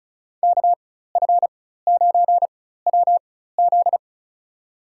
Morse code, heard as a keyed tone, about 36 wpm, spaced as for 16 wpm, 700 Hz. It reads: KF9WZ
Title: USA Call Signs - 5 Characters - 4x Character Spacing - 35wpm